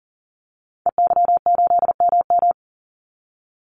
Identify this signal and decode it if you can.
Morse code, heard as a keyed tone, about 40 wpm, 700 Hz. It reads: EY8MM
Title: Call Signs Repeated 3x - 40wpm